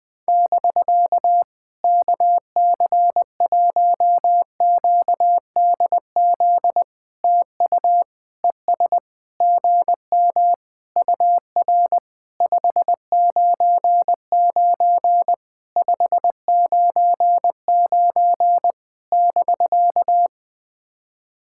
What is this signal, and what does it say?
Morse code, heard as a keyed tone, about 20 wpm, 700 Hz. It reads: <BK> KC1QDZ TU ES GM UR 599 599 <BK>
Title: POTA QSO Practice - 20wpm